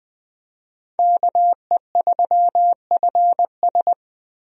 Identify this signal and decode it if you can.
Morse code, heard as a keyed tone, about 20 wpm, 700 Hz. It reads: KE3FS
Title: USA Call Signs - 5 Characters - Repeated 2x - 20wpm